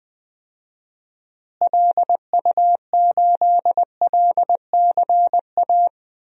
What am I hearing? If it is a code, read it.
Morse code, heard as a keyed tone, about 20 wpm, 700 Hz. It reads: LU8LCA
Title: DXCC-100 Call Signs - 6 Characters - Repeated 2x - 20wpm